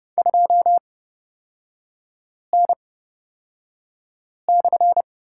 Morse code, heard as a keyed tone, about 30 wpm, spaced as for 5 wpm, 700 Hz. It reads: 2N/
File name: Lesson 36e - BT - Three Characters 30wpm